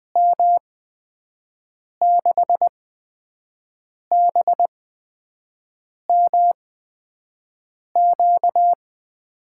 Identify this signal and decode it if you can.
Morse code, heard as a keyed tone, about 20 wpm, spaced as for 5 wpm, 700 Hz. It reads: M6BMQ
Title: USA Call Signs - 6 Characters - 8x Character Spacing - 20wpm